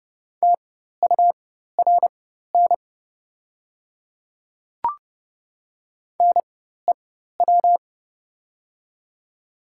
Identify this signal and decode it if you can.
Morse code, heard as a keyed tone, about 30 wpm, spaced as for 14 wpm, 700 Hz. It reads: TURN NEW
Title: 4x Character Spacing - Top 200 Words - 30wpm